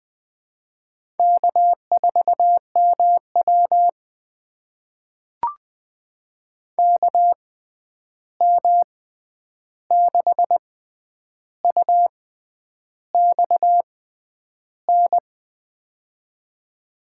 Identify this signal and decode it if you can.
Morse code, heard as a keyed tone, about 20 wpm, spaced as for 10 wpm, 700 Hz. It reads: K4MW KM6UX N
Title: USA Call Signs - 6x Character Spacing - 20wpm